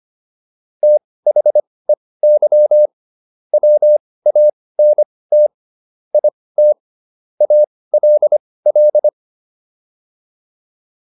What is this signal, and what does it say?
Morse code, heard as a keyed tone, about 25 wpm, spaced as for 18 wpm, 600 Hz. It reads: THEY WANT IT ALL